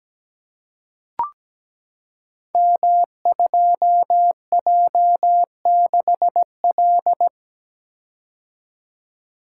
Morse code, heard as a keyed tone, about 17 wpm, 700 Hz. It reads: M2J6L
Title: Code Groups - 5 Letter:Numbers - 17wpm